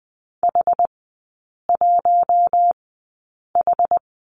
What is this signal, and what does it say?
Morse code, heard as a keyed tone, about 20 wpm, spaced as for 8 wpm, 700 Hz. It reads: H1H
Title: 060 Lesson 11e - Letter h - Three Characters 20wpm